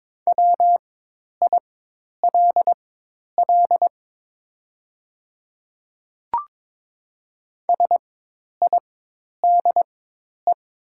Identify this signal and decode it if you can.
Morse code, heard as a keyed tone, about 22 wpm, spaced as for 10 wpm, 700 Hz. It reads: WILL SIDE
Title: 4x Character Spacing - Top 100 Words - 22wpm